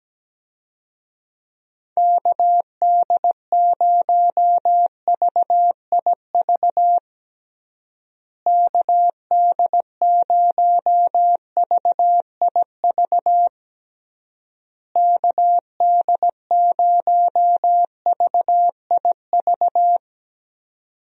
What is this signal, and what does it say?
Morse code, heard as a keyed tone, about 17 wpm, 700 Hz. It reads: KD0VIV KD0VIV KD0VIV
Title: USA Call Signs - 6 Characters - Repeated 3x - 17wpm